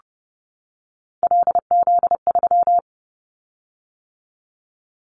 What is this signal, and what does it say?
Morse code, heard as a keyed tone, about 30 wpm, 700 Hz. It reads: LZ3
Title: Callsign Prefixes for Top 100 DXCC - 30wpm